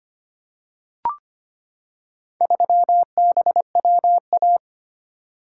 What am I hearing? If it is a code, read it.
Morse code, heard as a keyed tone, about 25 wpm, 700 Hz. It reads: 3BWA
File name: practice / 25wpm / Four Letter-Number Combinations - 25wpm